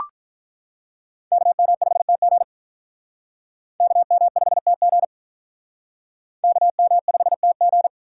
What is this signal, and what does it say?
Morse code, heard as a keyed tone, about 44 wpm, 700 Hz. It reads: KM5TG KM5TG KM5TG